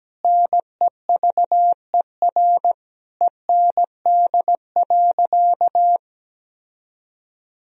Morse code, heard as a keyed tone, about 17 wpm, 700 Hz. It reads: NEVER END.
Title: Sentences from Top 1000 Words - Review Part 3 - 17wpm